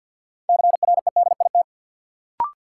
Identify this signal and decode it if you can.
Morse code, heard as a keyed tone, about 53 wpm, 700 Hz. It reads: CREDIT
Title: Sentences from Top 1000 Words - New Word Drill - 50wpm